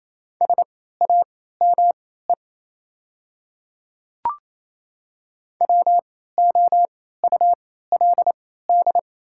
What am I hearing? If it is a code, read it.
Morse code, heard as a keyed tone, about 28 wpm, spaced as for 16 wpm, 690 Hz. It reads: SAME WOULD
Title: 3x Character Spacing - Top 200 Words - 28wpm